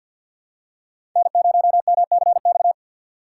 Morse code, heard as a keyed tone, about 52 wpm, 700 Hz. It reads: N0MCX
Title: USA Call Signs - 5 Characters - 2x Character Spacing - 50wpm